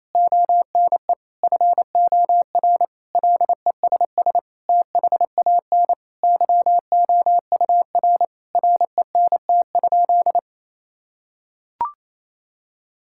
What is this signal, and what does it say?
Morse code, heard as a keyed tone, about 28 wpm, 700 Hz. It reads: ONE FOR LESS THAN YOUR RENT?
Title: Sentences from Top 1000 Words - Review Part 5 - 28wpm